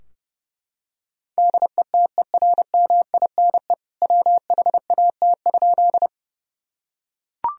Morse code, heard as a keyed tone, about 30 wpm, 700 Hz. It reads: DETERMINE WHAT?
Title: Sentences from Top 1000 Words - Review Part 4 - 30wpm